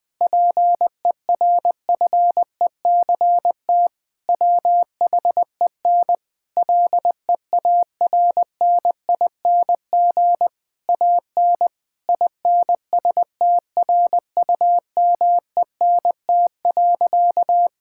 Morse code, heard as a keyed tone, about 21 wpm, 700 Hz. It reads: PERFECT WHEN LEARNING AN INSTRUMENT.
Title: Difficult Sentences with Hints - 20wpm